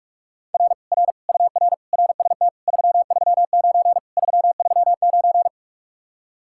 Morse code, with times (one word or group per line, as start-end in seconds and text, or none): 0.5-0.7 R
0.9-1.1 R
1.3-1.7 UR
1.9-2.5 RST
2.7-4.0 339
4.2-5.5 339